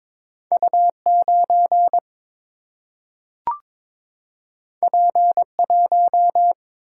Morse code, heard as a keyed tone, about 22 wpm, 700 Hz. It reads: U9 P1
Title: Single-Letter Single-Number 22wpm